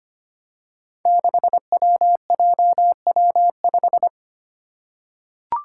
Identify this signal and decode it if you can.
Morse code, heard as a keyed tone, about 25 wpm, 700 Hz. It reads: 6WJW5
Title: Five Challenging Letter-Number Combinations 25wpm